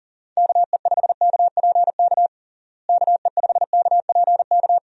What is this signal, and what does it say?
Morse code, heard as a keyed tone, about 37 wpm, 700 Hz. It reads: KE5KPK KE5KPK
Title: USA Call Signs - 6 Characters - Repeated 2x - 40wpm